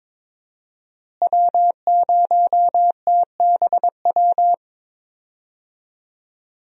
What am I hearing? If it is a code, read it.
Morse code, heard as a keyed tone, about 22 wpm, 700 Hz. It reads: W0TBW
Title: USA Call Signs - 5 Characters - Repeated 3x - 22wpm